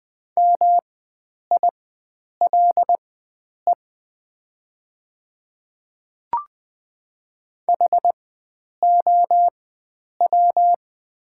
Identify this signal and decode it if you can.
Morse code, heard as a keyed tone, about 20 wpm, spaced as for 9 wpm, 700 Hz. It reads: MILE HOW
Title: 4x Character Spacing - Top 500 Words - 20wpm